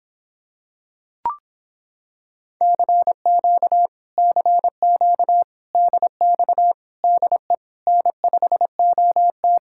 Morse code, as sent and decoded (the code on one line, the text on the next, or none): -.-. --.- / -.-. --.- / -.. -..- / -.. / -. ..... --- -
CQ CQ DX DE N5OT